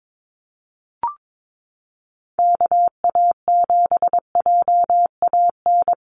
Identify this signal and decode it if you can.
Morse code, heard as a keyed tone, about 22 wpm, 700 Hz. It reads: KA7JAN
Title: USA Call Signs Repeated 2x 22wpm